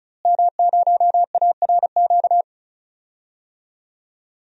Morse code, heard as a keyed tone, about 36 wpm, 700 Hz. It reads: M0ARQ
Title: DXCC-100 Call Signs - 5 Characters - Repeated 2x - 35wpm